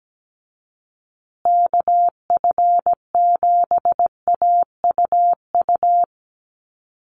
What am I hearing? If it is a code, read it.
Morse code, heard as a keyed tone, about 17 wpm, 700 Hz. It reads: KF7AUU
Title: USA Call Signs Repeated 3x 17wpm